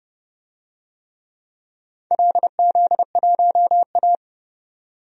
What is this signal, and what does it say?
Morse code, heard as a keyed tone, about 31 wpm, 700 Hz. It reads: LZ1A
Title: DXCC-100 Call Signs - 4 Characters - 30wpm